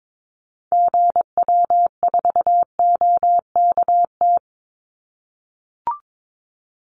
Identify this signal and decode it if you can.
Morse code, heard as a keyed tone, about 22 wpm, 700 Hz. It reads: GW4OKT
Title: Call Signs - 22wpm